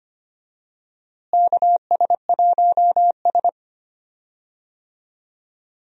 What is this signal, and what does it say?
Morse code, heard as a keyed tone, about 25 wpm, 700 Hz. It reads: KS1S
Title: USA Call Signs - 4 Characters - Repeated 2x - 25wpm